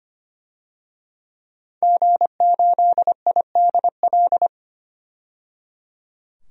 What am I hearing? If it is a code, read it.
Morse code, heard as a keyed tone, about 25 wpm, 700 Hz. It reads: G8IDL
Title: DXCC-100 Call Signs - 5 Characters - 25wpm